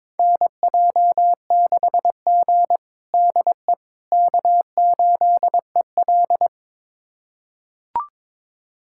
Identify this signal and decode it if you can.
Morse code, heard as a keyed tone, about 22 wpm, 700 Hz. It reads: NJ6G DE K8EL
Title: USA Callsign DE Callsign - 4 Characters - 22wpm